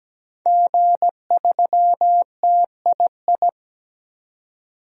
Morse code, heard as a keyed tone, about 17 wpm, 700 Hz. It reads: G3TII
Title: DXCC-100 Call Signs - 5 Characters - 17wpm